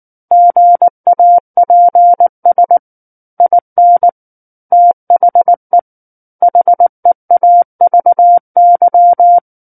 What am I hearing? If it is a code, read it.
Morse code, heard as a keyed tone, about 19 wpm, 700 Hz. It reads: GAPS IN THE HEAVY
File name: Book One / 10-Chapter 10 - 19 WPM